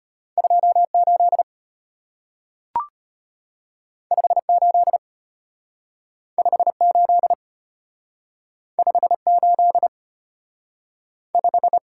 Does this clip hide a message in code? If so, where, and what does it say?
Yes: Morse code from 0.4 to 11.8 s, heard as a keyed tone, about 34 wpm, 700 Hz. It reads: 28 58 58 58 5